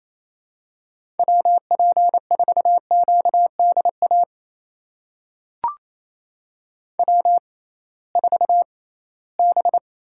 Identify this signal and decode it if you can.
Morse code, heard as a keyed tone, about 28 wpm, spaced as for 18 wpm, 700 Hz. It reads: WP4QDA W4B
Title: USA Call Signs - 6x Character Spacing - 28wpm